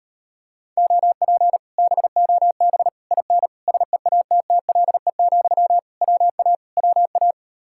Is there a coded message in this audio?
Yes: Morse code from 0.8 to 7.3 s, heard as a keyed tone, about 33 wpm, 700 Hz. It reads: OP BOB IN SEATTLE, WA WA